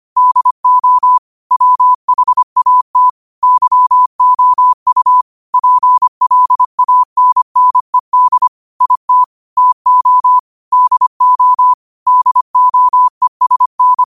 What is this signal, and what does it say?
Morse code, heard as a keyed tone, about 25 wpm, 1000 Hz. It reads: DO WHAT YOU PLANNED IT TO DO DOESN